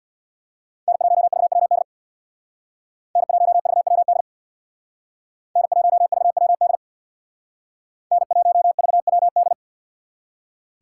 Morse code, heard as a keyed tone, about 64 wpm, 700 Hz. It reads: N1VWD N1VWD N1VWD N1VWD